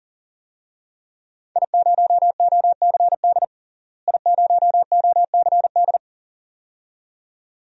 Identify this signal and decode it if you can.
Morse code, heard as a keyed tone, about 37 wpm, 700 Hz. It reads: I0OCD I0OCD
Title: DXCC-100 Call Signs - 5 Characters - Repeated 2x - 40wpm